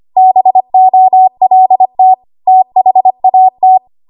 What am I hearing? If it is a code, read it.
Morse code, heard as a keyed tone, about 25 wpm, 750 Hz. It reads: BOLT THAT
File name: Archive / 25 WPM / 25WPM